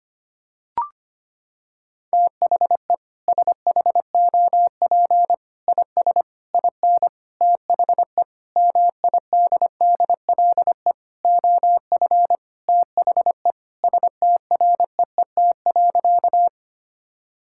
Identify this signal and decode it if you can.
Morse code, heard as a keyed tone, about 25 wpm, 700 Hz. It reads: THE SHOP IS IN THE MIDDLE OF THE STREET.
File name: practice / 25wpm / Sentences with 4-Word Phrases - 25wpm